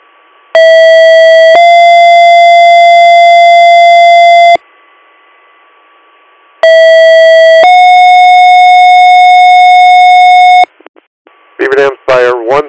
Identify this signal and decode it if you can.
Two-tone page, heard as a one-second tone, then a three-second tone, about 650 then 690 Hz, about 650 then 730 Hz, with speech. It is heard from 0.5 to 10.6 s.